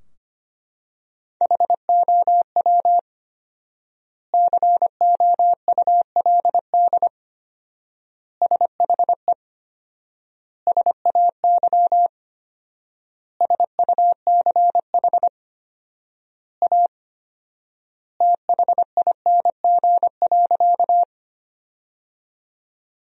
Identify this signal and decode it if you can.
Morse code, heard as a keyed tone, about 25 wpm, 700 Hz. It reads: HOW COULD SHE SAY SUCH A THING.
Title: 4x Word-Spacing - Sentences from Top 200 Words - 25wpm